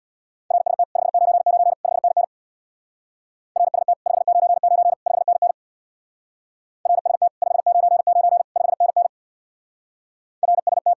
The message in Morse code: RST 599 5NN RST 599 5NN RST 599 5NN RST

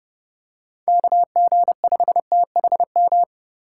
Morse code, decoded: KG5THM

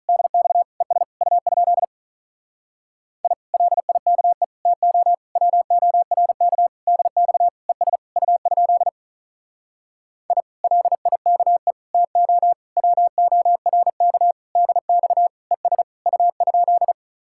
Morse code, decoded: DX ES U? I LIKE TO WORK DX ES U? I LIKE TO WORK DX ES U?

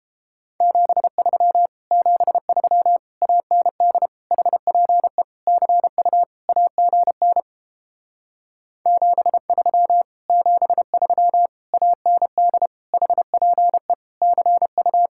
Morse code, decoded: 73 73 AND HPE CU AGN 73 73 AND HPE CU

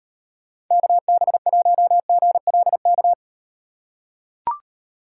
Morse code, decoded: KB1GLK

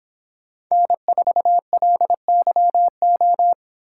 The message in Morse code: N4LYO

